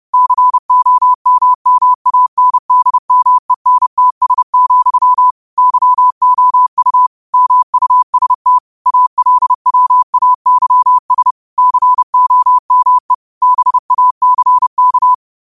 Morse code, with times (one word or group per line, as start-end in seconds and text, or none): 0.1-5.3 COMMANDMENTS,
5.6-7.1 YOU
7.3-8.6 MUST
8.9-11.3 ALWAYS
11.6-13.1 COME
13.4-15.1 BACK